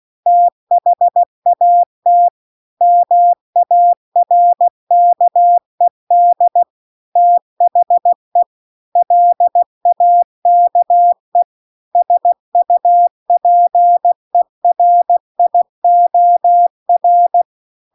Morse code, heard as a keyed tone, about 16 wpm, 700 Hz. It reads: THAT MARKED THE LAKE SUPERIOR